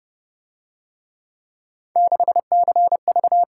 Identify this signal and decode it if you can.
Morse code, heard as a keyed tone, about 29 wpm, 700 Hz. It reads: 6CV